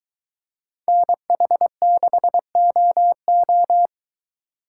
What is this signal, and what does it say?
Morse code, heard as a keyed tone, about 23 wpm, 700 Hz. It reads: NH6OO